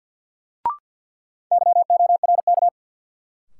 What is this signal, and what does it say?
Morse code, heard as a keyed tone, about 48 wpm, 700 Hz. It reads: YORK
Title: Sets of 2 Words - New Word Drill - 50wpm